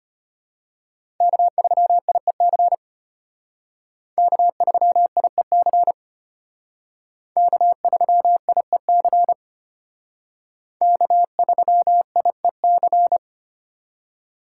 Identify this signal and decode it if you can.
Morse code, heard as a keyed tone, about 33 wpm, 700 Hz. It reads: K3IEC K3IEC K3IEC K3IEC